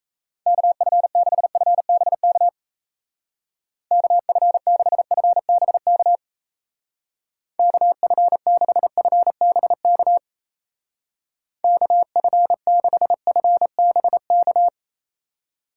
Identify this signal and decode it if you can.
Morse code, heard as a keyed tone, about 37 wpm, 700 Hz. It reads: KF6FBK KF6FBK KF6FBK KF6FBK